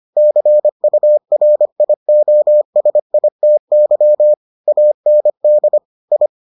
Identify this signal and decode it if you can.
Morse code, heard as a keyed tone, about 25 wpm, 600 Hz. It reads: CURIOSITY AND I